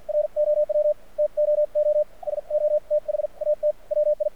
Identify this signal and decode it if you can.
Morse code, heard as a keyed tone, about 50 wpm, spaced as for 43 wpm, 600 Hz, with radio noise. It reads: ROW TOO SO THAT WI
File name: archived / CRSnet